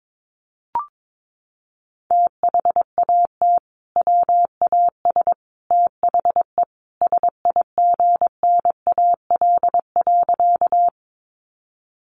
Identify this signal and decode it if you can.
Morse code, heard as a keyed tone, about 22 wpm, 700 Hz. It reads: THAT WAS THE SIGNAL.
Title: Sentences from Top 1400 Words 22wpm